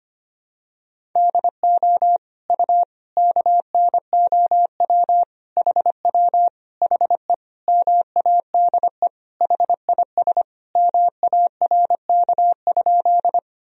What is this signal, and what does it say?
Morse code, heard as a keyed tone, about 25 wpm, 700 Hz. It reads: DO U KNOW HW HE MADE HIS MARK?
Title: Sentences from Top 300 Words and Abbreviations 25wpm